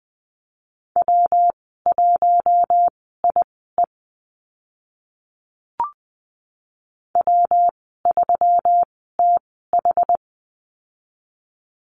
Morse code, heard as a keyed tone, about 20 wpm, spaced as for 14 wpm, 700 Hz. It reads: W1IE W3TH
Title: 140 Lesson 23g - Letter f - Call Signs 20wpm